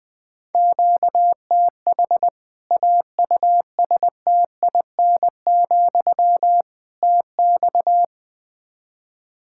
Morse code, heard as a keyed tone, about 20 wpm, 700 Hz. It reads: QTH AUSTIN, TX
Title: Practice QSO - Exchange 1 - Beginning - 20wpm